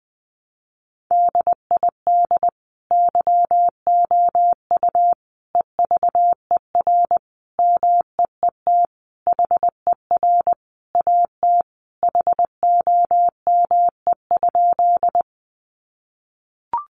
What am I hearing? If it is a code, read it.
Morse code, heard as a keyed tone, about 20 wpm, 700 Hz. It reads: DID YOU EVER MEET HER AT HOME?
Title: Sentences from Top 1100 Words 20wpm